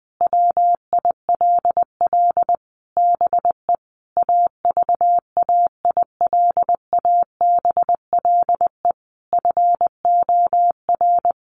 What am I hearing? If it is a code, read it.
Morse code, heard as a keyed tone, about 19 wpm, 700 Hz. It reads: WILL BE AVAILABLE FOR
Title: Sentences from Top 1400 Words 20wpm